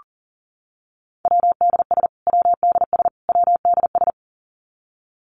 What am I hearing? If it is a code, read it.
Morse code, heard as a keyed tone, about 40 wpm, 700 Hz. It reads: WDS WDS WDS
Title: Abbreviations Repeated 3x - 40wpm